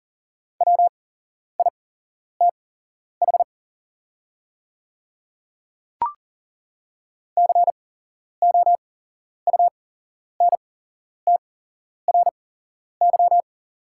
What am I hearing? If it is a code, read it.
Morse code, heard as a keyed tone, about 40 wpm, spaced as for 11 wpm, 700 Hz. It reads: WITH COUNTRY
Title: 8x Character Spacing - Top 500 Words - 40wpm